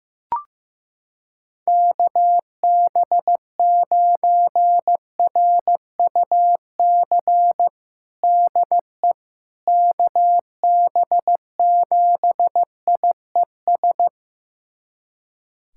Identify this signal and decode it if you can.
Morse code, heard as a keyed tone, about 15 wpm, 700 Hz. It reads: KB9RUC DE KB7IES
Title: USA Callsign DE Callsign - 6 Characters - 15wpm